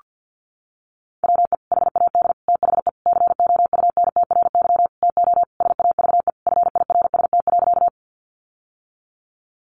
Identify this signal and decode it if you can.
Morse code, heard as a keyed tone, about 49 wpm, 700 Hz. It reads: WE HAD THE COUNTRY TO SAVE FIRST.
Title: Sentences from Top 1100 Words 50wpm